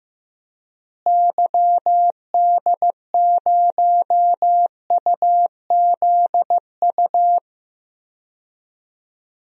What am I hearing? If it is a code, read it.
Morse code, heard as a keyed tone, about 15 wpm, 700 Hz. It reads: YD0UZU
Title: DXCC-100 Call Signs - 6 Characters - Repeated 3x - 15wpm